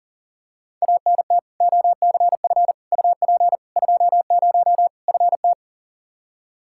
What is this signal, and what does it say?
Morse code, heard as a keyed tone, about 39 wpm, 700 Hz. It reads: ANT OCF UP 20 FT
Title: Practice QSO - Exchange 2 - Info - 40wpm